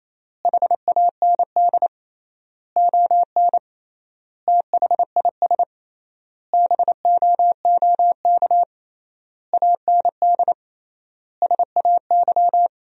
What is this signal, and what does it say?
Morse code, heard as a keyed tone, about 28 wpm, 700 Hz. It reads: HAND ON THIS BOOK AND SAY
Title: 3x Word-Spacing - Sentences from Top 400 Words - 28wpm